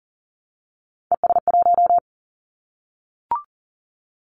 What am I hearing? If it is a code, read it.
Morse code, heard as a keyed tone, about 40 wpm, 700 Hz. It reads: ES1
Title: Callsign Prefixes for Top 100 DXCC - 40wpm